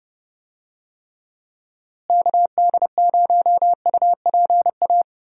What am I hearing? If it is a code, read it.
Morse code, heard as a keyed tone, about 30 wpm, 700 Hz. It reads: KD0UPA
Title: USA Call Signs - 6 Characters - Repeated 2x - 30wpm